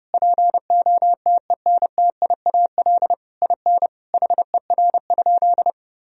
Morse code, heard as a keyed tone, about 30 wpm, 700 Hz. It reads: POTENTIAL IN HER?
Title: Sentences from Top 1000 Words - Review Part 4 - 30wpm